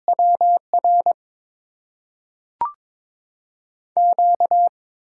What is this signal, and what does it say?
Morse code, heard as a keyed tone, about 22 wpm, spaced as for 3 wpm, 700 Hz. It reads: W RQ
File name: Top 90 QSO Elements - 6x Character Spacing - 22wpm